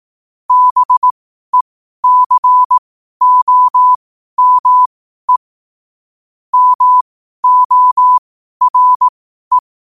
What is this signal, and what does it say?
Morse code, heard as a keyed tone, about 18 wpm, spaced as for 12 wpm, 1000 Hz. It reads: BECOME MORE
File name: qotdfiles / 12wpm